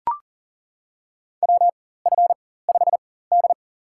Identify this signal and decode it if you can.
Morse code, heard as a keyed tone, about 40 wpm, spaced as for 19 wpm, 700 Hz. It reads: WF5D
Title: USA Call Signs - 4 Characters - 4x Character Spacing - 40wpm